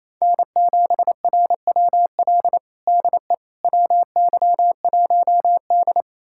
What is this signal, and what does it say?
Morse code, heard as a keyed tone, about 28 wpm, 700 Hz. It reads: N7RWL DE WY1D
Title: USA Callsign DE Callsign 28wpm